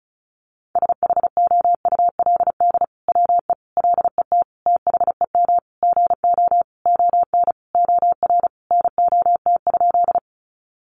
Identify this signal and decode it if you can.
Morse code, heard as a keyed tone, about 34 wpm, 700 Hz. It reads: SHOULD WE LET THEM GO ON OR NOT?
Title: Sentences from Top 600 Words 35wpm